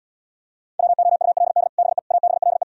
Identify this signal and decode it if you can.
Morse code, heard as a keyed tone, about 77 wpm, 700 Hz. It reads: WOULD BE ABLE